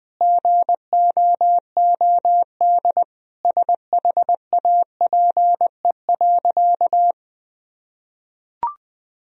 Morse code, heard as a keyed tone, about 21 wpm, 700 Hz. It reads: GOOD SHAPE.
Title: Sentences from Top 1000 Words - Review Part 3 - 20wpm